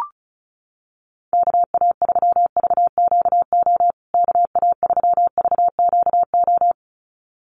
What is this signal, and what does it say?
Morse code, heard as a keyed tone, about 34 wpm, 700 Hz. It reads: KA3VQO KA3VQO
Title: USA Call Signs Repeated 2x 35wpm